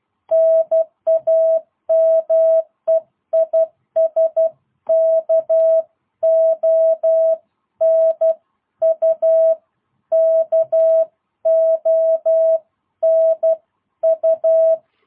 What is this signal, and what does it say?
Morse code, heard as a keyed tone, about 12 wpm, 660 Hz. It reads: NAME IS KONU KONU
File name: jh8cbh / 2022cw